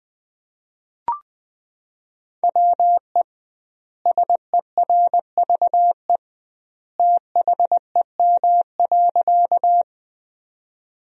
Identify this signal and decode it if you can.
Morse code, heard as a keyed tone, about 20 wpm, 700 Hz. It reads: WE SERVE THEM.
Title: 2x Word-Spacing - Sentences from Top 400 Words - 20wpm